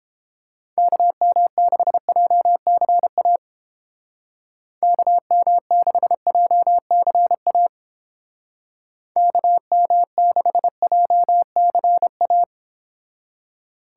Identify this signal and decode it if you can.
Morse code, heard as a keyed tone, about 30 wpm, 700 Hz. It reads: KM6JCA KM6JCA KM6JCA